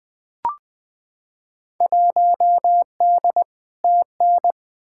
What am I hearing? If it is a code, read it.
Morse code, heard as a keyed tone, about 21 wpm, 700 Hz. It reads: 1D TN
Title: Field Day Exchanges - 20wpm